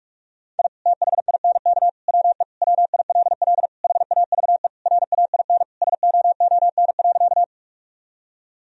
Morse code, heard as a keyed tone, about 48 wpm, 700 Hz. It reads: I THINK WE WILL HAVE RAIN SOON.
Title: Sentences from Top 1000 Words - Review Part 1 - 45wpm